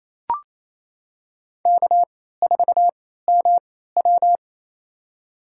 Morse code, heard as a keyed tone, about 28 wpm, spaced as for 16 wpm, 700 Hz. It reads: K4MW
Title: USA Call Signs - 4 Characters - 3x Character Spacing - 28wpm